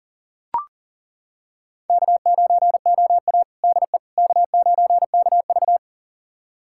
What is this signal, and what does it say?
Morse code, heard as a keyed tone, about 38 wpm, 700 Hz. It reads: K9OA DE K9KV
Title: USA Callsign DE Callsign - 4 Characters - 40wpm